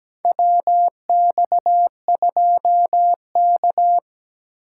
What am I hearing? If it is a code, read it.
Morse code, heard as a keyed tone, about 17 wpm, 700 Hz. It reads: WX2K
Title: USA Call Signs - 4 Characters - 2x Character Spacing - 17wpm